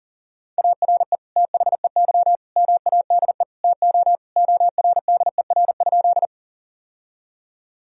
Morse code, heard as a keyed tone, about 40 wpm, 700 Hz. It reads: ARE THEY MADE TO ORDER?
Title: Sentences from Top 1000 Words - Review Part 2 - 40wpm